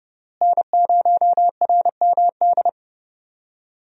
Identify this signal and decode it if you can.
Morse code, heard as a keyed tone, about 32 wpm, 700 Hz. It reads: N0RMD
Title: USA Call Signs - 5 Characters - Repeated 3x - 30wpm